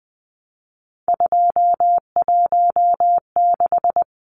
Morse code, heard as a keyed tone, about 20 wpm, 700 Hz. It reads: 216
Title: Numbers - 3-Digit 20wpm